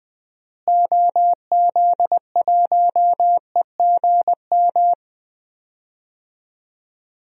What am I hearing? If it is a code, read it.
Morse code, heard as a keyed tone, about 19 wpm, 700 Hz. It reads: OZ1EGM
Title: DXCC-100 Call Signs - 6 Characters - Repeated 3x - 20wpm